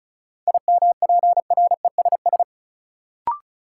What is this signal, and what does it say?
Morse code, heard as a keyed tone, about 36 wpm, 700 Hz. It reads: IMPRESS
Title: Sentences from Top 1600 Words - New Word Drill - 35wpm